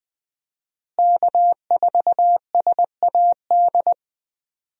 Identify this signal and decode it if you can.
Morse code, heard as a keyed tone, about 20 wpm, 700 Hz. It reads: K4SAD
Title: USA Call Signs - 5 Characters - Repeated 3x - 20wpm